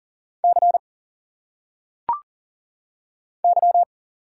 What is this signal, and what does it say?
Morse code, heard as a keyed tone, about 40 wpm, 700 Hz. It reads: C Y
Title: Single Letters - Phonetic Pronunciation 40wpm